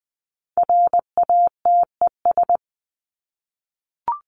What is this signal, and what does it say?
Morse code, heard as a keyed tone, about 20 wpm, 700 Hz. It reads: RATES
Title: ES:S Suffix - Top 100 Words with Base Words 20wpm